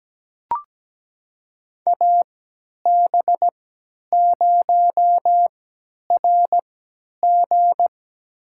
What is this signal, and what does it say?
Morse code, heard as a keyed tone, about 17 wpm, spaced as for 10 wpm, 700 Hz. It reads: AB0RG
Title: USA Call Signs - 3x Character Spacing - 17wpm